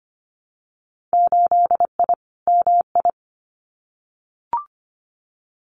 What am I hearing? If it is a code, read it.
Morse code, heard as a keyed tone, about 25 wpm, 700 Hz. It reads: 8I MI
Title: Winter Field Day Exchanges Encore 25wpm